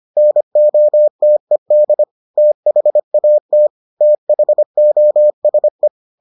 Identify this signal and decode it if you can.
Morse code, heard as a keyed tone, about 25 wpm, 600 Hz. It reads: NOTED THAT THOSE